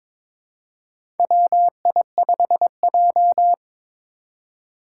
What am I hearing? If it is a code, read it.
Morse code, heard as a keyed tone, about 22 wpm, 700 Hz. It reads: WI5J
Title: Four Challenging Letter-Number Combinations 22wpm